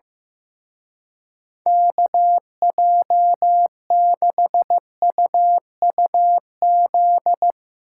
Morse code, heard as a keyed tone, about 16 wpm, 700 Hz. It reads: KJ6UUZ
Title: Speed Racing - USA Call Signs 15wpm